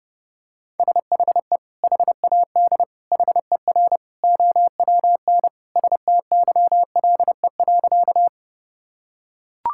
Morse code, heard as a keyed tone, about 30 wpm, 700 Hz. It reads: SHE HAD HER OWN STYLE.
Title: Sentences from Top 1000 Words - Review Part 5 - 30wpm